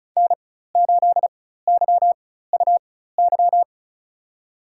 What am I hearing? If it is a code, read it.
Morse code, heard as a keyed tone, about 34 wpm, spaced as for 16 wpm, 700 Hz. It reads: N8YUY
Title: USA Call Signs - 4x Character Spacing - 35wpm